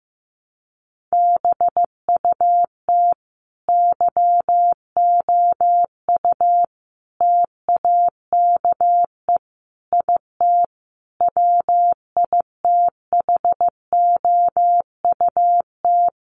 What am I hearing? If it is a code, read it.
Morse code, heard as a keyed tone, about 15 wpm, 700 Hz. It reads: BUT YOU TAKE IT WITHOUT
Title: Sentences from Top 1200 Words 15wpm